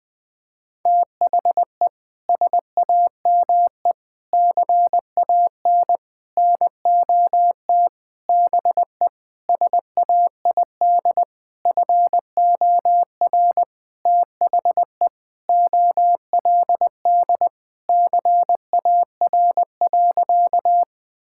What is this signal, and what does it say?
Morse code, heard as a keyed tone, about 20 wpm, 700 Hz. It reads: THE SAME CAN NOT BE SAID FOR THE OLD CAR.
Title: Sentences with 8-Word Phrases - 20wpm